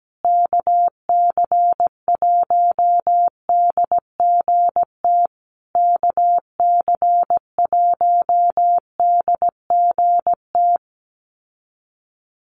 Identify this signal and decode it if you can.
Morse code, heard as a keyed tone, about 17 wpm, 700 Hz. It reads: KC1DGT KC1DGT
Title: USA Call Signs Repeated 3x 17wpm